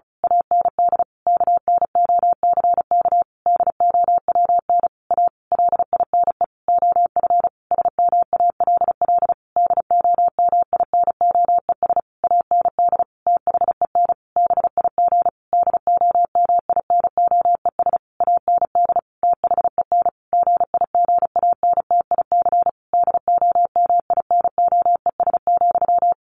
Morse code, heard as a keyed tone, about 34 wpm, 700 Hz. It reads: AND KNOCK DOWN A LINE OF SMALL DOMINOES AND THEN BIG DOMINOES AND THEN GIGANTIC DOMINOES,
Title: A Sound Of Thunder 35wpm